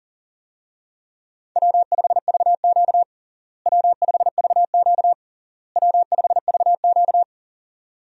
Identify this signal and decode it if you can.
Morse code, heard as a keyed tone, about 40 wpm, 700 Hz. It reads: W5VQ W5VQ W5VQ
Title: USA Call Signs - 4 Characters - Repeated 3x - 40wpm